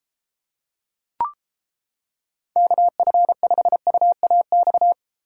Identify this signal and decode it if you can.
Morse code, heard as a keyed tone, about 31 wpm, 700 Hz. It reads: KF5UAX